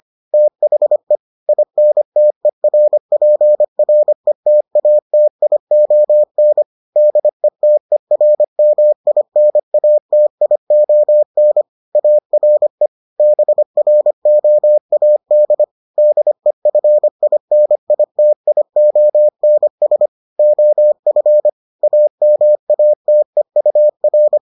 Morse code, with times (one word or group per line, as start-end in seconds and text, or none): 0.3-1.2 THE
1.5-6.6 INTERPRETATION
7.0-11.6 DETERMINATION
12.0-12.9 ARE
13.2-15.6 BROAD
16.0-20.1 DEFINITIONS
20.4-21.5 OF
21.8-24.4 AMATEUR